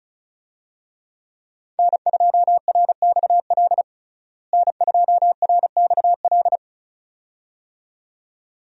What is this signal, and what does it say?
Morse code, heard as a keyed tone, about 36 wpm, 700 Hz. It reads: N2RXL N2RXL
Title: USA Call Signs - 5 Characters - Repeated 2x - 35wpm